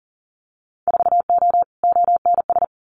Morse code, heard as a keyed tone, about 41 wpm, 700 Hz. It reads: 4O ONS
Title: Winter Field Day Exchanges 40wpm